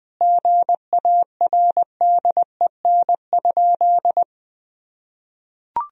Morse code, heard as a keyed tone, about 21 wpm, 700 Hz. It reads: GARDEN?
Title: Sentences from Top 1000 Words - Review Part 4 - 20wpm